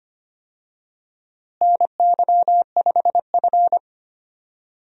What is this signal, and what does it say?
Morse code, heard as a keyed tone, about 25 wpm, 700 Hz. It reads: NY5F
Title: USA Call Signs - 4 Characters - Repeated 3x - 25wpm